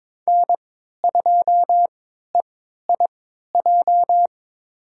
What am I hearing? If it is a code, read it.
Morse code, heard as a keyed tone, about 22 wpm, spaced as for 13 wpm, 700 Hz. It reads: N2EIJ
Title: USA Call Signs - 5 Characters - 3x Character Spacing - 22wpm